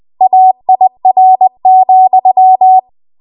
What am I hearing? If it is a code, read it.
Morse code, heard as a keyed tone, about 20 wpm, 750 Hz. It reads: AIR,
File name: Archive / 20 WPM / 20WPM